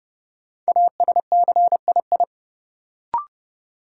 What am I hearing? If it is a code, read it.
Morse code, heard as a keyed tone, about 30 wpm, 700 Hz. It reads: ASCII
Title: Sets of 5 Words - New Word Drill - 30wpm